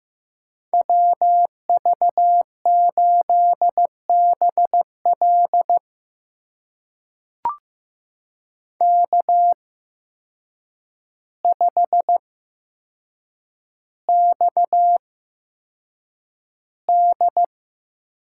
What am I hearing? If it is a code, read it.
Morse code, heard as a keyed tone, about 15 wpm, spaced as for 7 wpm, 700 Hz. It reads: WV8BL K5XD
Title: USA Call Signs - 8x Character Spacing - 15wpm